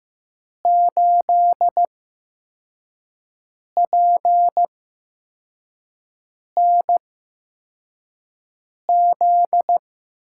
Morse code, heard as a keyed tone, about 15 wpm, spaced as for 4 wpm, 700 Hz. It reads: 8PNZ
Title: USA Call Signs - 5 Characters - 8x Character Spacing - 15wpm